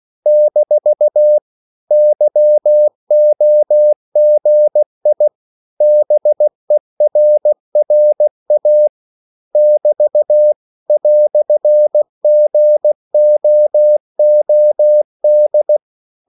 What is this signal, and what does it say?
Morse code, heard as a keyed tone, about 16 wpm, 600 Hz. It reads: - YOGI BERRA <BT> "GOOD